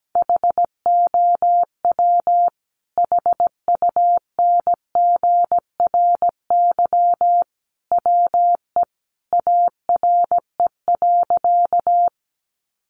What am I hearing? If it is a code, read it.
Morse code, heard as a keyed tone, about 17 wpm, 700 Hz. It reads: HOW HUNGRY WE ARE.
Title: Sentences from Top 1400 Words 17wpm